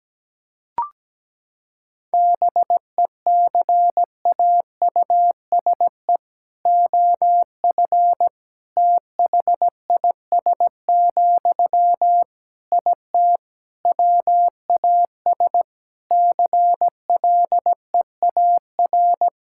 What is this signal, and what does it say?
Morse code, heard as a keyed tone, about 17 wpm, 700 Hz. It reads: BECAUSE OF THIS, IT WAS CLEAR
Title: Sentences from Top 1000 Words - Review Part 6 - 17wpm